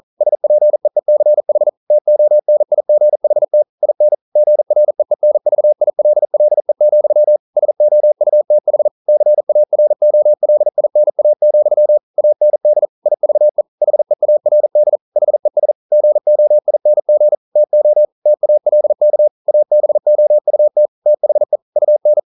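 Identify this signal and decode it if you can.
Morse code, heard as a keyed tone, about 42 wpm, 600 Hz. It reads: SPEECH TONIGHT IN GREENVILLE, SOUTH CAROLINA, AND IVE HEARD HES GOING TO TALK ABOUT THE UN
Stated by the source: Morse code practice at 41 wpm, from CW med Gnister.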